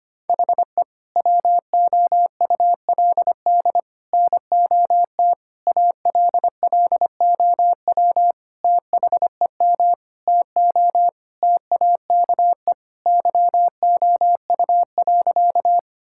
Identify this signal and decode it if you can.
Morse code, heard as a keyed tone, about 25 wpm, 700 Hz. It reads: HE WOULD NOT ALLOW THEM TO TAKE YOU.
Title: Sentences from Top 1000 Words - Review Part 5 - 25wpm